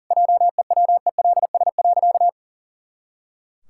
Morse code, heard as a keyed tone, about 40 wpm, 700 Hz. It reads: JEWELS.
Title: Pangram Challenge v2 - 40wpm